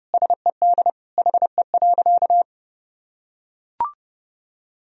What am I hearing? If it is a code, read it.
Morse code, heard as a keyed tone, about 32 wpm, 700 Hz. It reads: SED HE.
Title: Sentences from Top 400 Words and Abbreviations 30wpm